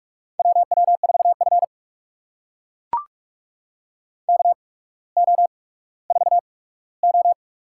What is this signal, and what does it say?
Morse code, heard as a keyed tone, about 41 wpm, spaced as for 19 wpm, 700 Hz. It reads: WW4F KO4O
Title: USA Call Signs - 8x Character Spacing - 45wpm